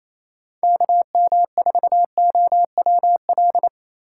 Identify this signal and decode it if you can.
Morse code, heard as a keyed tone, about 34 wpm, 700 Hz. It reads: KM4OWL
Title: Speed Racing - USA Call Signs 28wpm